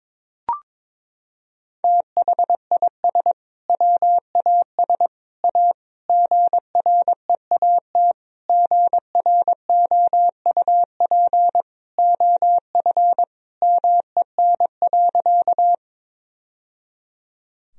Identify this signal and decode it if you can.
Morse code, heard as a keyed tone, about 22 wpm, 700 Hz. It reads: THIS WAS A GREAT GROUP OF MEN.
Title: Sentences from Top 1000 Words - Review Part 1 - 22wpm